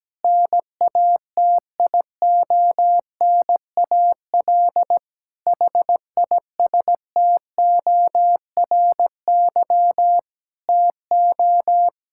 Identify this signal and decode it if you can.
Morse code, heard as a keyed tone, about 17 wpm, 700 Hz. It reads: NATIONAL HISTORY TO
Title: Sentences from Top 1000 Words - Review Part 6 - 17wpm